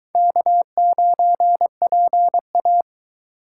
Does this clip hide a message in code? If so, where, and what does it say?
Yes: Morse code from 0.2 to 2.8 s, heard as a keyed tone, about 22 wpm, 700 Hz. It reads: K9PA